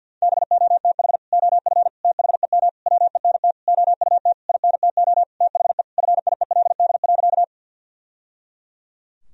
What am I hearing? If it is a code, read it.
Morse code, heard as a keyed tone, about 50 wpm, 700 Hz. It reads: BOTH OF THEM WENT OUT INTO THE FIELD.
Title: Sentences from Top 1000 Words - Review Part 2 - 50wpm